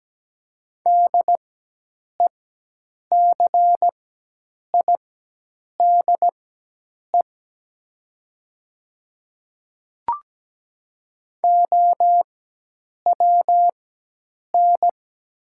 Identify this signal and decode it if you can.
Morse code, heard as a keyed tone, about 17 wpm, spaced as for 8 wpm, 700 Hz. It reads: DECIDE OWN
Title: 4x Character Spacing - Top 500 Words - 17wpm